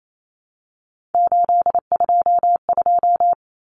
Morse code, heard as a keyed tone, about 28 wpm, 700 Hz. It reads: 822
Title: Common Cut Numbers - 3-Digit 28wpm